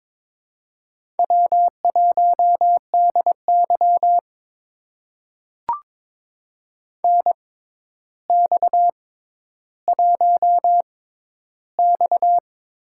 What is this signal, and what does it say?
Morse code, heard as a keyed tone, about 22 wpm, spaced as for 11 wpm, 700 Hz. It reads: W1DY NX1X
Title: USA Call Signs - 4 Characters - 6x Character Spacing - 22wpm